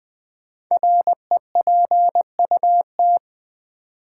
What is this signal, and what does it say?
Morse code, heard as a keyed tone, about 20 wpm, 700 Hz. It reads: REPUT
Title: Build A Word - 5 Letters At A Time - 10-Letter Words - 20wpm